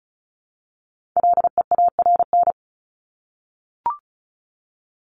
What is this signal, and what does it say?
Morse code, heard as a keyed tone, about 34 wpm, 700 Hz. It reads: LEARN
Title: ING Suffix - Top 100 Words with Base Words 35wpm